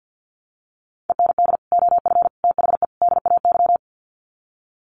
Morse code, heard as a keyed tone, about 50 wpm, 700 Hz. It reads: END OF THE DAY
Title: Sets of 4 Words 50wpm